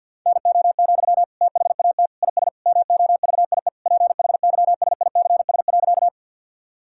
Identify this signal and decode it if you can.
Morse code, heard as a keyed tone, about 47 wpm, 700 Hz. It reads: NO, THAT IS MOVIE PHYSICS.
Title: Sentences from Top 1000 Words - Review Part 5 - 50wpm